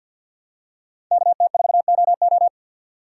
Morse code, heard as a keyed tone, about 53 wpm, 700 Hz. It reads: KT4OO
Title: USA Call Signs - 3x Character Spacing - 50wpm